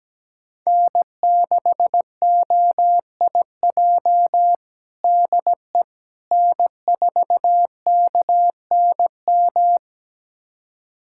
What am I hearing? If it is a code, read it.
Morse code, heard as a keyed tone, about 17 wpm, 700 Hz. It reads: N6OIJ DE N4KNM
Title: USA Callsign DE Callsign - 5 Characters - 17wpm